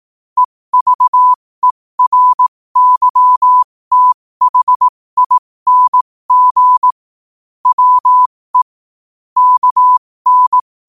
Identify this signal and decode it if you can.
Morse code, heard as a keyed tone, about 18 wpm, spaced as for 15 wpm, 1000 Hz. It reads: EVERYTHING WE KN
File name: qotdfiles / 15wpm